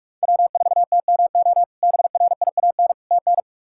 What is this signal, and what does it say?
Morse code, heard as a keyed tone, about 46 wpm, 700 Hz. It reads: W4TMO BRIAN TN
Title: K1USN SST QSO Practice - 45wpm